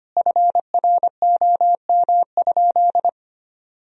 Morse code, heard as a keyed tone, about 25 wpm, 700 Hz. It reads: FROM?
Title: Sentences from Top 1000 Words - Review Part 4 - 25wpm